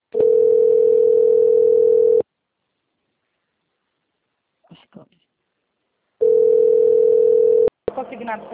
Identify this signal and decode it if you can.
Ringback tone (North America), 440 + 480 Hz, with speech.
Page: Cordeiropolis - SP